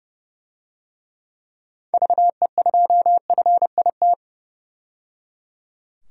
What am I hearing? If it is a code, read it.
Morse code, heard as a keyed tone, about 30 wpm, 700 Hz. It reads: VE2FIT
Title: DXCC-100 Call Signs - 6 Characters - 30wpm